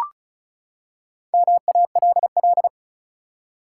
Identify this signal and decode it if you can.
Morse code, heard as a keyed tone, about 36 wpm, 700 Hz. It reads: MALL
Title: Four Letter Words - 35wpm